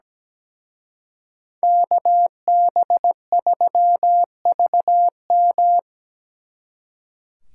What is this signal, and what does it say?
Morse code, heard as a keyed tone, about 20 wpm, 700 Hz. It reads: KB3VM
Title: Speed Racing - USA 2x2 Call Signs 17wpm